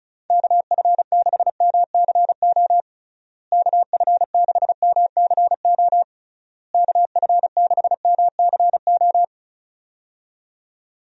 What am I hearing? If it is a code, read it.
Morse code, heard as a keyed tone, about 34 wpm, 700 Hz. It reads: KF6MCO KF6MCO KF6MCO
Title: USA Call Signs - 6 Characters - Repeated 3x - 35wpm